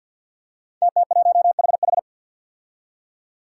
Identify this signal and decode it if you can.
Morse code, heard as a keyed tone, about 46 wpm, 700 Hz. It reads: TT1HH
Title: Five Challenging Letter-Number Combinations 50wpm